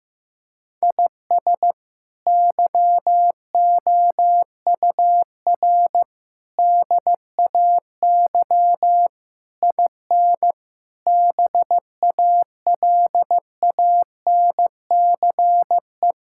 Morse code, heard as a keyed tone, about 16 wpm, 700 Hz. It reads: IS YOUR DAY IN BALANCE
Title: Sentences from Top 1000 Words - Review Part 6 - 15wpm